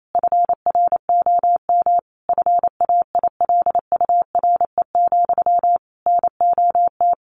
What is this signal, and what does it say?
Morse code, heard as a keyed tone, about 28 wpm, 700 Hz. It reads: FROM FAILURE, NOT
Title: Sentences from Top 900 Words 28wpm